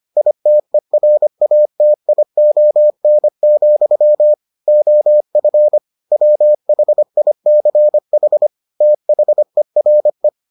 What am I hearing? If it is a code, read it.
Morse code, heard as a keyed tone, about 25 wpm, 600 Hz. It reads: ITERATION, OF WHICH THERE